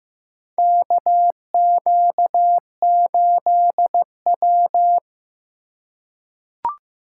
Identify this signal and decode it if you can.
Morse code, heard as a keyed tone, about 15 wpm, 700 Hz. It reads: KQ8W
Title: USA Call Signs - 4 Characters - 4x Character Spacing - 15wpm